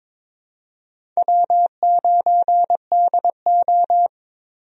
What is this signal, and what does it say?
Morse code, heard as a keyed tone, about 21 wpm, 700 Hz. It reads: W9DO
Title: USA Call Signs - 4 Characters - Repeated 3x - 22wpm